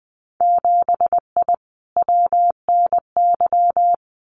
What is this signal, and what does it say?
Morse code, heard as a keyed tone, about 20 wpm, 700 Hz. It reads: 7I WNY
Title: Winter Field Day Exchanges Encore 20wpm